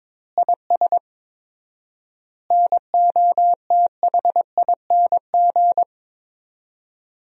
Morse code, heard as a keyed tone, about 22 wpm, 700 Hz. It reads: IS NOTHING
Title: 4x Word-Spacing - Sentences from Top 500 Words - 22wpm